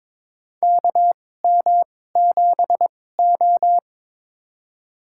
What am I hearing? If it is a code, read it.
Morse code, heard as a keyed tone, about 22 wpm, spaced as for 16 wpm, 700 Hz. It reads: KM7O
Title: USA Call Signs - 4 Characters - 2x Character Spacing - 22wpm